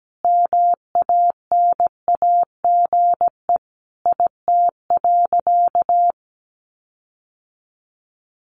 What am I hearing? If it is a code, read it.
Morse code, heard as a keyed tone, about 17 wpm, 700 Hz. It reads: MANAGE IT.
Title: Sentences from Top 1400 Words 17wpm